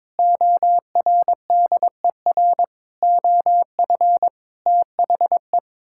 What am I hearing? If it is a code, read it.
Morse code, heard as a keyed tone, about 22 wpm, 700 Hz. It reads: ORDER OF THE
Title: Sentences from Top 1000 Words - Review Part 5 - 22wpm